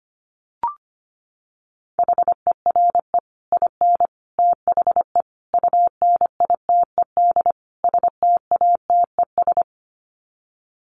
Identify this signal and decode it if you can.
Morse code, heard as a keyed tone, about 25 wpm, 700 Hz. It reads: HERE IN THE UNITED STATES
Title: Sets of 5 Words 25wpm